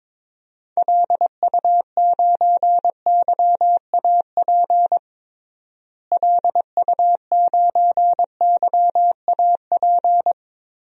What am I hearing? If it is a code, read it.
Morse code, heard as a keyed tone, about 22 wpm, 700 Hz. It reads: LU9YAP LU9YAP
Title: DXCC-100 Call Signs - 6 Characters - Repeated 3x - 22wpm